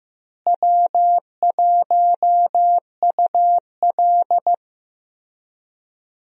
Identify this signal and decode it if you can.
Morse code, heard as a keyed tone, about 15 wpm, 700 Hz. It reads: W1UL
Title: USA Call Signs - 4 Characters - Repeated 3x - 15wpm